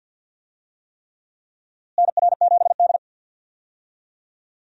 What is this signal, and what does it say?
Morse code, heard as a keyed tone, about 46 wpm, 700 Hz. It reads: NR7D
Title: USA Call Signs - 4 Characters - Repeated 3x - 50wpm